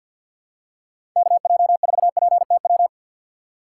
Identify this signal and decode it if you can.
Morse code, heard as a keyed tone, about 52 wpm, 700 Hz. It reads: KJ4PTW
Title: USA Call Signs - 2x Character Spacing - 50wpm